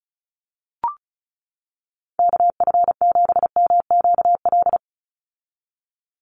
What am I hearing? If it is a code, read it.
Morse code, heard as a keyed tone, about 34 wpm, 700 Hz. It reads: KF7MQL
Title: USA Call Signs 35wpm